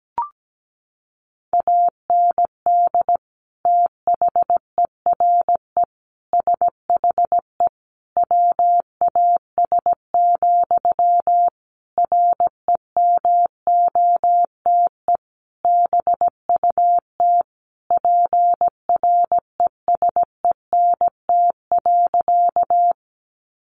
Morse code, heard as a keyed tone, about 17 wpm, 700 Hz. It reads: AND THERE SHE WAS, REMOTE BUT PRESENT.
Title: Sentences from Top 1300 Words 17wpm